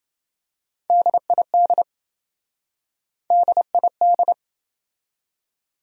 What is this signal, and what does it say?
Morse code, heard as a keyed tone, about 27 wpm, 700 Hz. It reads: DID DID